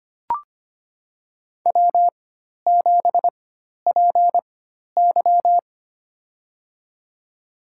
Morse code, heard as a keyed tone, about 25 wpm, spaced as for 12 wpm, 690 Hz. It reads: W7PY
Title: USA Call Signs - 4 Characters - 4x Character Spacing - 25wpm